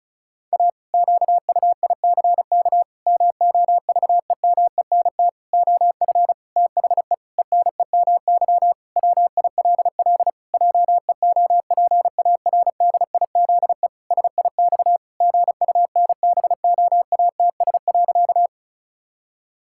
Morse code, heard as a keyed tone, about 34 wpm, 700 Hz. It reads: A QUICK MOVEMENT OF THE ENEMY WILL JEOPARDIZE SIX GUNBOATS.
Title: Pangram Challenge v2 - 35wpm